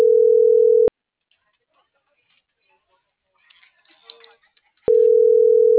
Ringback tone (North America), 440 + 480 Hz.